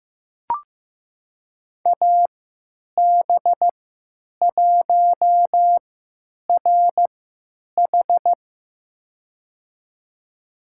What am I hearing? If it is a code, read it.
Morse code, heard as a keyed tone, about 15 wpm, spaced as for 9 wpm, 700 Hz. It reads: AB1RH
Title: USA Call Signs - 3x Character Spacing - 15wpm